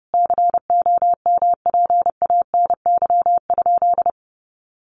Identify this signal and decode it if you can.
Morse code, heard as a keyed tone, about 30 wpm, 700 Hz. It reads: COMPANY?
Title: Sentences from Top 600 Words 30wpm